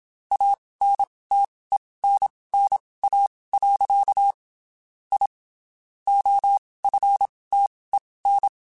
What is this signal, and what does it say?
Morse code, heard as a keyed tone, about 27 wpm, spaced as for 19 wpm, 800 Hz. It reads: ANTENNA. I OFTEN
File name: (spaced) / QSOs Spaced-mp3s / kc7sac @25 sp